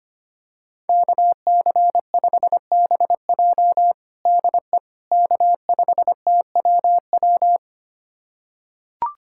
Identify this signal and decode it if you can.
Morse code, heard as a keyed tone, about 25 wpm, 700 Hz. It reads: KC5BJ DE K5TWW
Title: USA Callsign DE Callsign - 5 Characters - 25wpm